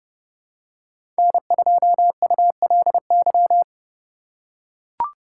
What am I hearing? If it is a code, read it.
Morse code, heard as a keyed tone, about 30 wpm, 700 Hz. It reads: N2ULY
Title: USA Call Signs - 5 Characters - 3x Character Spacing - 30wpm